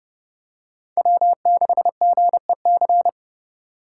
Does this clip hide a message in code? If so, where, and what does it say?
Yes: Morse code from 1.0 to 3.1 s, heard as a keyed tone, about 29 wpm, 700 Hz. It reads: W6GEC